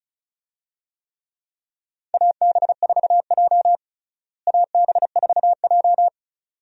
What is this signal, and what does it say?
Morse code, heard as a keyed tone, about 36 wpm, 700 Hz. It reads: AB4J AB4J
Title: USA Call Signs - 4 Characters - Repeated 2x - 35wpm